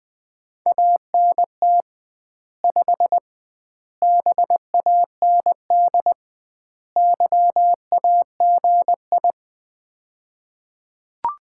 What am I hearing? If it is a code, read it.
Morse code, heard as a keyed tone, about 21 wpm, 700 Hz. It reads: ANT 5 BAND YAGI
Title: Common QSO Phrases - 2x Word-Spacing - 20wpm